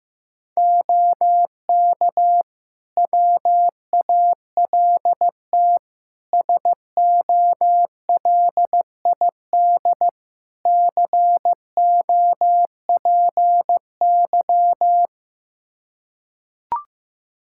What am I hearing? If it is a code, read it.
Morse code, heard as a keyed tone, about 15 wpm, 700 Hz. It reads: OK WALT SOLID COPY
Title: Practice QSO - Exchange 3 - More Info - 15wpm